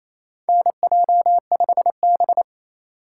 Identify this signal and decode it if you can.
Morse code, heard as a keyed tone, about 28 wpm, 700 Hz. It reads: NJ5B
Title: USA Call Signs - 2x Character Spacing - 28wpm